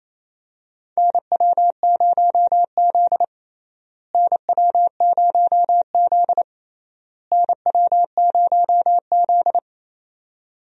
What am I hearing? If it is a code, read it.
Morse code, heard as a keyed tone, about 28 wpm, 700 Hz. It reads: NW0Z NW0Z NW0Z
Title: USA Call Signs - 4 Characters - Repeated 3x - 28wpm